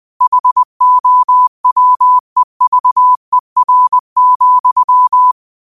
Morse code, heard as a keyed tone, about 20 wpm, 1000 Hz. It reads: HOWEVER,
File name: qotdfiles / 20wpm